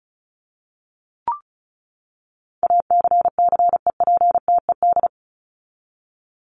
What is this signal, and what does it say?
Morse code, heard as a keyed tone, about 34 wpm, 700 Hz. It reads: ACCEPTED
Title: ED Suffix - Top 100 Words with Base Words 35wpm